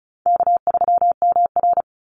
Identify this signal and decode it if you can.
Morse code, heard as a keyed tone, about 34 wpm, 700 Hz. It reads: K3MR
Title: Call Signs Repeated 3x - 35wpm